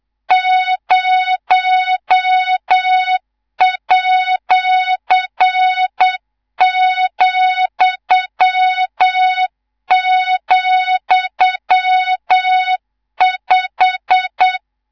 Morse code, heard as a keyed tone, about 8 wpm, 3700 Hz, with noise. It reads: 0@,,5
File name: Morze-HA2MN / hang / 19 GYAKORLAT